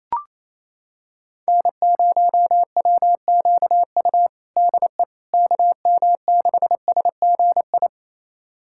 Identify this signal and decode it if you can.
Morse code, heard as a keyed tone, about 28 wpm, 700 Hz. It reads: N0WQU DE KM6SGI
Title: USA Callsign DE Callsign 28wpm